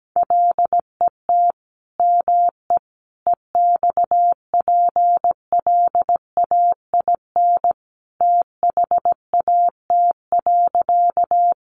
Morse code, heard as a keyed tone, about 17 wpm, 700 Hz. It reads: LET ME EXPLAIN THAT.
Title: Sentences from Top 1100 Words 17wpm